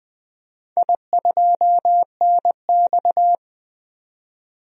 Morse code, heard as a keyed tone, about 21 wpm, 700 Hz. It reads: I2NX
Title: DXCC-100 Call Signs - 4 Characters - 20wpm